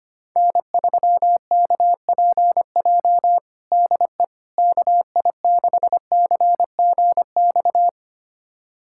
Morse code, heard as a keyed tone, about 25 wpm, 700 Hz. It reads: N3KPJ DE KI6CGX
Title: USA Callsign DE Callsign 25wpm